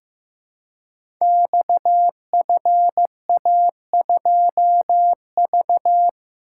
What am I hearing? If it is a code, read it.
Morse code, heard as a keyed tone, about 15 wpm, 700 Hz. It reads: XFA2V
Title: Code Groups - 5 Letter:Numbers - 15wpm